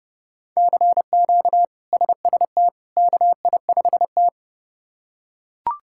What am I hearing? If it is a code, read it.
Morse code, heard as a keyed tone, about 32 wpm, 700 Hz. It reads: CQ SST KI5T
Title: K1USN SST QSO Practice - 30wpm